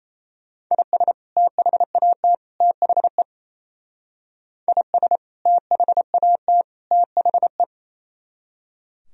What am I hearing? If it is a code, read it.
Morse code, heard as a keyed tone, about 33 wpm, 700 Hz. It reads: IS THAT THE IS THAT THE
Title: Speed-Racing - Sets of 3 Words 28wpm